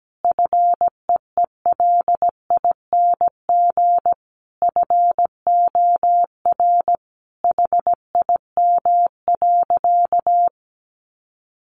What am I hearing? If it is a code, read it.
Morse code, heard as a keyed tone, about 17 wpm, 700 Hz. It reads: FEELING FOR HIM.
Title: Sentences from Top 1200 Words 17wpm